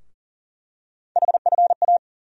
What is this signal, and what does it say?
Morse code, heard as a keyed tone, about 40 wpm, 700 Hz. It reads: HFA
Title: USA Call Sign Postfixes - 3 Letters 40wpm